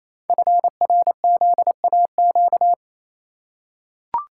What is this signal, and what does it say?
Morse code, heard as a keyed tone, about 27 wpm, 700 Hz. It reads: FRZAQ
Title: Code Groups - 5 Letters - 28wpm